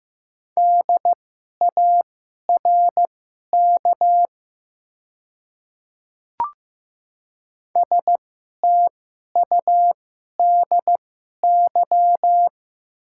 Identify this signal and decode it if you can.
Morse code, heard as a keyed tone, about 15 wpm, spaced as for 11 wpm, 700 Hz. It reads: DARK STUDY
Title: 2x Character Spacing - Top 500 Words - 15wpm